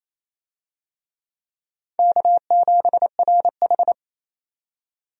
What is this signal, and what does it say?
Morse code, heard as a keyed tone, about 28 wpm, 700 Hz. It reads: K7RH
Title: USA Call Signs - 4 Characters - 28wpm